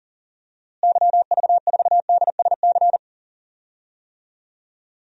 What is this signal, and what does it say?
Morse code, heard as a keyed tone, about 39 wpm, 700 Hz. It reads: YV4DSC
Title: DXCC-100 Call Signs - 6 Characters - Repeated 2x - 40wpm